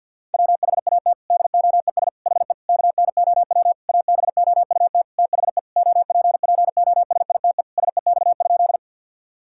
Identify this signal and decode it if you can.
Morse code, heard as a keyed tone, about 50 wpm, 700 Hz. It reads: WHAT DOES HE KNOW ABOUT THE OPPOSITE SEX?
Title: Sentences from Top 1600 Words 50wpm